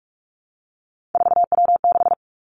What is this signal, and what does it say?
Morse code, heard as a keyed tone, about 46 wpm, 700 Hz. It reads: 4W6
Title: Three Challenging Letter-Number Combinations 45wpm